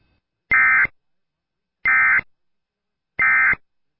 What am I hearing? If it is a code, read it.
Emergency Alert System (SAME) tones, heard as harsh buzzing bursts. It is heard from 0.5 to 3.6 s.